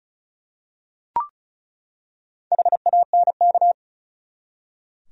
Morse code, heard as a keyed tone, about 36 wpm, 700 Hz. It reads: HANK
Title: USA Common Names - Max 5 Letters - 35wpm